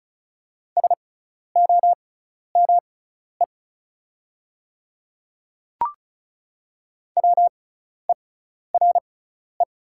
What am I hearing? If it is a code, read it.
Morse code, heard as a keyed tone, about 34 wpm, spaced as for 12 wpm, 700 Hz. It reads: SOME WERE
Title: 6x Character Spacing - Top 100 Words - 35wpm